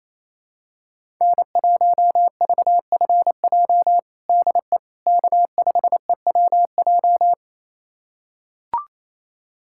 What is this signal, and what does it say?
Morse code, heard as a keyed tone, about 27 wpm, 700 Hz. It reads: N1VFJ DE K5EWJ
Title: USA Callsign DE Callsign - 5 Characters - 28wpm